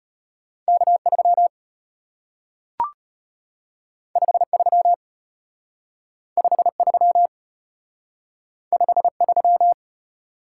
Morse code, heard as a keyed tone, about 34 wpm, 700 Hz. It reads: K3 53 53 53